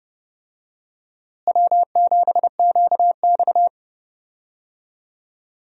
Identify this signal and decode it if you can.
Morse code, heard as a keyed tone, about 31 wpm, 700 Hz. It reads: W7QX
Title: USA Call Signs - 4 Characters - Repeated 2x - 30wpm